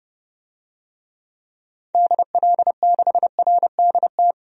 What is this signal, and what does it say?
Morse code, heard as a keyed tone, about 32 wpm, 700 Hz. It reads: DL6RDT
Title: DXCC-100 Call Signs - 6 Characters - 30wpm